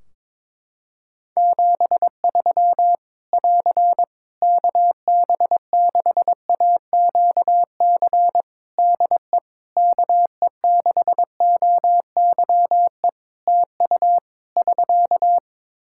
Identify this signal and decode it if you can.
Morse code, heard as a keyed tone, about 22 wpm, 700 Hz. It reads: 73 <AR> KB6AQC DE KE6OYE TU <SK>
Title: Practice QSO - Exchange 4 - Ending - 22wpm